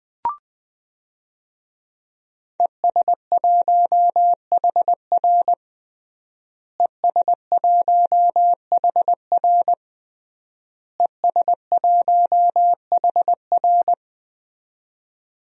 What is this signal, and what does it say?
Morse code, heard as a keyed tone, about 20 wpm, 700 Hz. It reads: ES1HR ES1HR ES1HR
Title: DXCC-100 Call Signs - 5 Characters - Repeated 3x - 20wpm